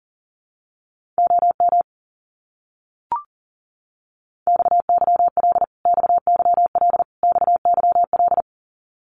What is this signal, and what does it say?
Morse code, heard as a keyed tone, about 40 wpm, 700 Hz. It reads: OM XYL XYL XYL
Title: Abbreviations Repeated 3x - 40wpm